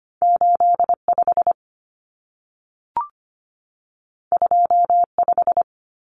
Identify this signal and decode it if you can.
Morse code, heard as a keyed tone, about 25 wpm, 700 Hz. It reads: 85 25
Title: Common Cut Numbers - 2-Digit 25wpm